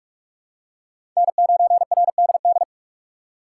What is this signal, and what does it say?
Morse code, heard as a keyed tone, about 47 wpm, 700 Hz. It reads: N9RDD
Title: USA Call Signs - 5 Characters - 3x Character Spacing - 45wpm